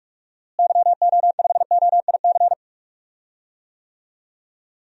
Morse code, heard as a keyed tone, about 46 wpm, 700 Hz. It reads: YO5OIC
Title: DXCC-100 Call Signs - 6 Characters - Repeated 2x - 45wpm